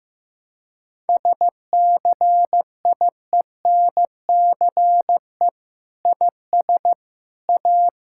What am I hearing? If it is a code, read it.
Morse code, heard as a keyed tone, about 16 wpm, 700 Hz. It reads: SCIENCE IS A
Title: Sentences from Top 1000 Words - Review Part 6 - 15wpm